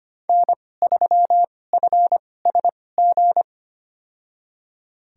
Morse code, heard as a keyed tone, about 25 wpm, spaced as for 18 wpm, 700 Hz. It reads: N3FSG
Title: USA Call Signs - 2x Character Spacing - 25wpm